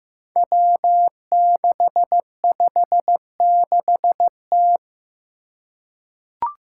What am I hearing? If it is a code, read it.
Morse code, heard as a keyed tone, about 15 wpm, 700 Hz. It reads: W656T
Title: Five Challenging Letter-Number Combinations 15wpm